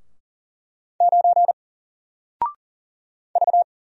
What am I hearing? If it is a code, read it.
Morse code, heard as a keyed tone, about 40 wpm, 700 Hz. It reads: 9 V
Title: Single Letter-Number 40wpm